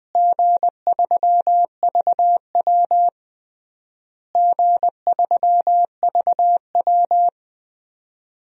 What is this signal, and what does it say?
Morse code, heard as a keyed tone, about 21 wpm, 700 Hz. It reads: G3VW G3VW
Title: DXCC-100 Call Signs - 4 Characters - Repeated 3x - 20wpm